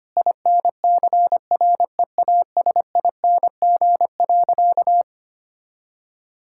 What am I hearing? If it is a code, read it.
Morse code, heard as a keyed tone, about 25 wpm, 700 Hz. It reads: INCREASING.
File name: practice / 25wpm / Sentences with 2-Word Phrases - 25wpm